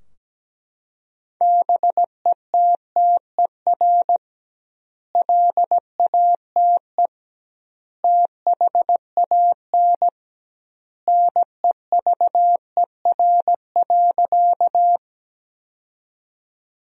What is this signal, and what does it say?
Morse code, heard as a keyed tone, about 17 wpm, 700 Hz. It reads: BETTER LATE THAN NEVER.
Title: 2x Word-Spacing - Sentences from Top 400 Words - 17wpm